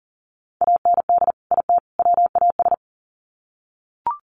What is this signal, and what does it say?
Morse code, heard as a keyed tone, about 40 wpm, 700 Hz. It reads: AND IT WAS
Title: Sets of 3 Words 40wpm